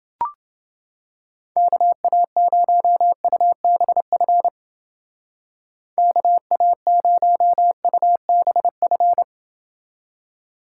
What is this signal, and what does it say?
Morse code, heard as a keyed tone, about 28 wpm, 700 Hz. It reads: KA0UBF KA0UBF